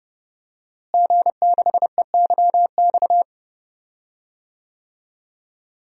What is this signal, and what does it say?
Morse code, heard as a keyed tone, about 31 wpm, 700 Hz. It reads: G6EYX
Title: DXCC-100 Call Signs - 5 Characters - Repeated 3x - 30wpm